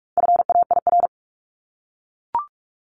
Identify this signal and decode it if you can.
Morse code, heard as a keyed tone, about 44 wpm, 700 Hz. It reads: FAIR
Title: LY Suffix - Top 100 Words with Base Words 45wpm